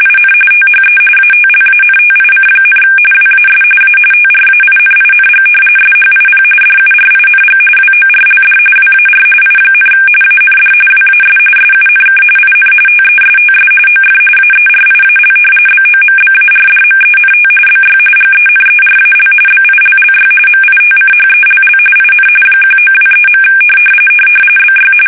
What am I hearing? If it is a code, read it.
RTTY, about 45.45 baud. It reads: IN THE ATLANTIC, WAS A TRAINING SHIP DURING WORLD WAR I, AND WAS SUNK IN TESTS OF AERIAL BOMBARDMENT BY GENERAL BILLY MITCHELL IN 1923. BY
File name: fsk850 / battleshipnj